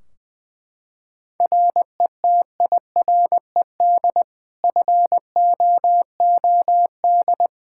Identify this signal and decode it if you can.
Morse code, heard as a keyed tone, about 21 wpm, 700 Hz. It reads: RETIRED FOOD
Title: K1USN SST QSO Practice 20wpm